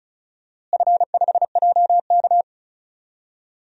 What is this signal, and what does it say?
Morse code, heard as a keyed tone, about 36 wpm, 700 Hz. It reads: F5JK
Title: DXCC-100 Call Signs - 4 Characters - Repeated 3x - 35wpm